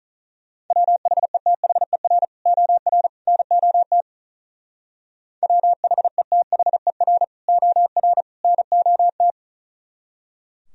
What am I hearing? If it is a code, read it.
Morse code, heard as a keyed tone, about 44 wpm, 700 Hz. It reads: WHETHER OR NOT WHETHER OR NOT
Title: Speed-Racing - Sets of 3 Words 35wpm